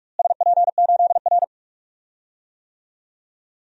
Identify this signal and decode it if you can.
Morse code, heard as a keyed tone, about 42 wpm, 700 Hz. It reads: SP8R
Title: DXCC-100 Call Signs - 4 Characters - Repeated 3x - 45wpm